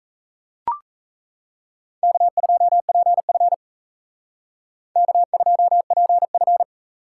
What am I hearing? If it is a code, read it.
Morse code, heard as a keyed tone, about 36 wpm, 700 Hz. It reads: K2PF K2PF